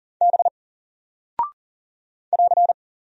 Morse code, heard as a keyed tone, about 40 wpm, 700 Hz. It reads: B <AR>
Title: Single Character (Letter:Number:Punctuation:Prosign) 40wpm